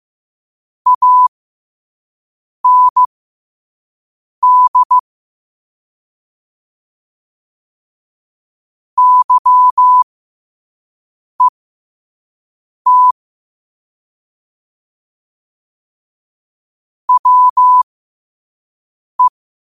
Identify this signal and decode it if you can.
Morse code, heard as a keyed tone, about 15 wpm, spaced as for 5 wpm, 1000 Hz. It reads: AND YET WE
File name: qotdfiles / 5wpm